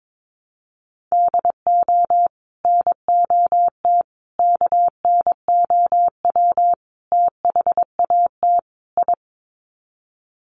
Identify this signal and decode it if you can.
Morse code, heard as a keyed tone, about 22 wpm, 700 Hz. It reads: DO NOT KNOW THAT I
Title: Sets of 5 Words Encore2 22wpm